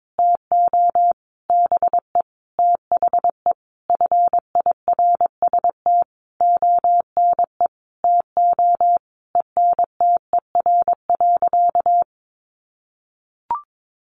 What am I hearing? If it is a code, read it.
Morse code, heard as a keyed tone, about 22 wpm, 700 Hz. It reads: TO BE THE FIRST ONE TO ENTER.
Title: Sentences from Top 1300 Words 22wpm